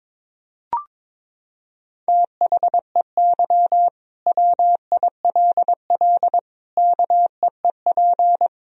Morse code, heard as a keyed tone, about 22 wpm, 700 Hz. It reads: THEY WILL KEEP
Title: Sentences from Top 1000 Words - Review Part 3 - 22wpm